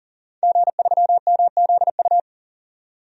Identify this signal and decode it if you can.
Morse code, heard as a keyed tone, about 40 wpm, 700 Hz. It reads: G3MZU
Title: DXCC-100 Call Signs - 5 Characters - 40wpm